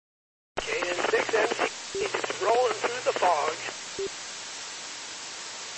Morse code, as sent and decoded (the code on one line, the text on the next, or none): none